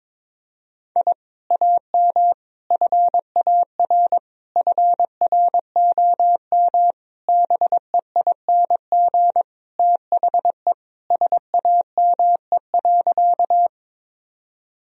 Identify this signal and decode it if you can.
Morse code, heard as a keyed tone, about 22 wpm, 700 Hz. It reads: I AM FAR FROM BEING THE SAME.
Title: Sentences from Top 1000 Words - Review Part 3 - 22wpm